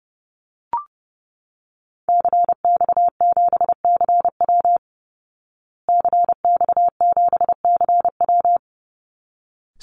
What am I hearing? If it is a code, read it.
Morse code, heard as a keyed tone, about 28 wpm, 700 Hz. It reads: CX7CW CX7CW
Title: Call Signs Repeated 3x - 30wpm